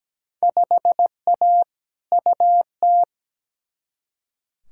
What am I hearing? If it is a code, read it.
Morse code, heard as a keyed tone, about 17 wpm, 700 Hz. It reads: 5A UT
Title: Field Day Exchanges - 17wpm